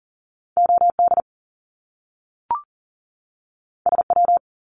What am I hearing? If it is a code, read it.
Morse code, heard as a keyed tone, about 40 wpm, 700 Hz. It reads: OD SW
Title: Top Two Letter Combinations in Words 40wpm